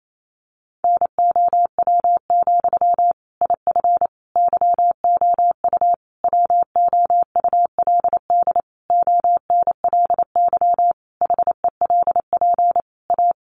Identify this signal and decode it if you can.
Morse code, heard as a keyed tone, about 28 wpm, 700 Hz. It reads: NOW, IF YOU WOULD ONLY HELP A
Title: Sentences from Top 900 Words 28wpm